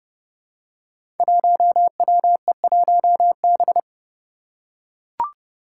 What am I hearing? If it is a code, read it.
Morse code, heard as a keyed tone, about 30 wpm, 700 Hz. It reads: 1WE1B
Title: Five Challenging Letter-Number Combinations 30wpm